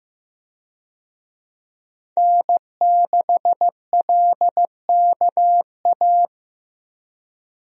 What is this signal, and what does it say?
Morse code, heard as a keyed tone, about 15 wpm, 700 Hz. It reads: N6LKA
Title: USA Call Signs - 5 Characters - Repeated 3x - 15wpm